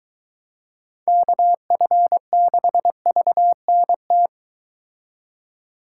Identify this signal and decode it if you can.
Morse code, heard as a keyed tone, about 23 wpm, 700 Hz. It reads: KF6VNT